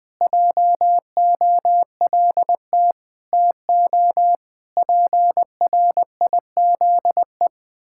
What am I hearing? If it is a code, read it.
Morse code, heard as a keyed tone, about 21 wpm, 700 Hz. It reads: JOLT TO PRIZE
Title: Pangram Challenge v2 - 20wpm